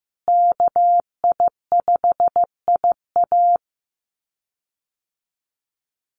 Morse code, heard as a keyed tone, about 15 wpm, 700 Hz. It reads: KI5IA
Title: USA Call Signs Repeated 3x 15wpm